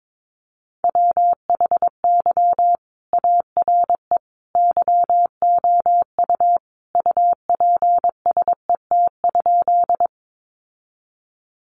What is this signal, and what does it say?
Morse code, heard as a keyed tone, about 22 wpm, 700 Hz. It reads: WHY ARE YOU UPSET?
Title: Sentences from Top 800 Words 22wpm